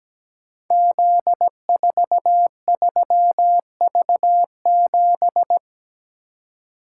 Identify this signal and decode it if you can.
Morse code, heard as a keyed tone, about 17 wpm, 700 Hz. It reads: Z43V7
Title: Code Groups - 5 Letter:Numbers - 17wpm